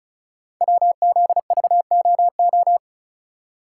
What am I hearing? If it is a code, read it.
Morse code, heard as a keyed tone, about 36 wpm, 700 Hz. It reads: WZVOO
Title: Code Groups - 5 Letter:Numbers - 35wpm